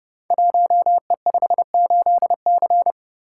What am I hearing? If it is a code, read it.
Morse code, heard as a keyed tone, about 30 wpm, 700 Hz. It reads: 1E58C
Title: Code Groups - 5 Letter:Numbers - 30wpm